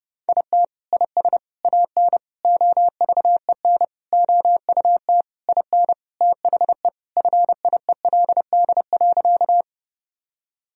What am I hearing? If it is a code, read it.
Morse code, heard as a keyed tone, about 30 wpm, 700 Hz. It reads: IT IS AN OVEN OUT IN THE FIELD.
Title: Sentences from Top 1000 Words - Review Part 4 - 30wpm